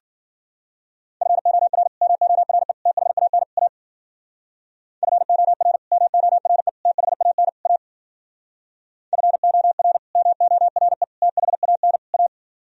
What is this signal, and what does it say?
Morse code, heard as a keyed tone, about 51 wpm, 700 Hz. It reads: FOR MORE THAN A FOR MORE THAN A FOR MORE THAN A